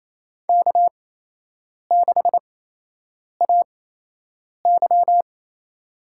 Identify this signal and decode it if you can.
Morse code, heard as a keyed tone, about 28 wpm, spaced as for 8 wpm, 700 Hz. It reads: K6AY
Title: USA Call Signs - 4 Characters - 8x Character Spacing - 28wpm